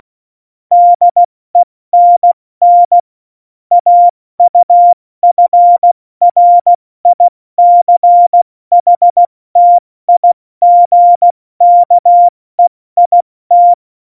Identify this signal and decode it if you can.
Morse code, heard as a keyed tone, about 16 wpm, 700 Hz. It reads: DENN AUFRICHTIGKEIT
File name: Morsekurs / Witze / Witz 11. Juni 2019, 70 F80